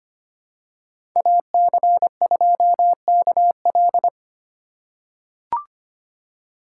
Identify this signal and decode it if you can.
Morse code, heard as a keyed tone, about 25 wpm, 700 Hz. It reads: AC2KL
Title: USA Call Signs - 2x Character Spacing - 25wpm